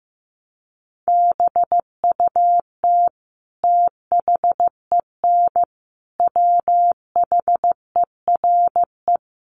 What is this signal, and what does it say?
Morse code, heard as a keyed tone, about 15 wpm, 700 Hz. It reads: BUT THEN WHERE
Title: Sentences from Top 1300 Words 15wpm